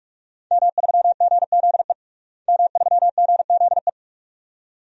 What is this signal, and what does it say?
Morse code, heard as a keyed tone, about 46 wpm, 700 Hz. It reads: M3GZE M3GZE
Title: DXCC-100 Call Signs - 5 Characters - Repeated 2x - 45wpm